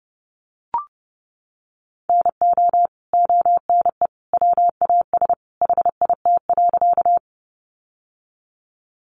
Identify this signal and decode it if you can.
Morse code, heard as a keyed tone, about 30 wpm, 700 Hz. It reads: NO ONE WAS HIT.
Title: Sentences from Top 1200 Words 30wpm